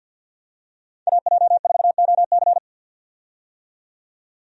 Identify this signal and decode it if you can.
Morse code, heard as a keyed tone, about 49 wpm, 700 Hz. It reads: AJ4OC
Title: USA Call Signs - 5 Characters - 50wpm